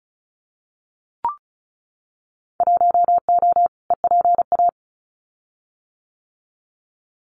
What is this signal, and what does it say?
Morse code, heard as a keyed tone, about 36 wpm, 700 Hz. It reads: 1O EPA
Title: Winter Field Day Exchanges 35wpm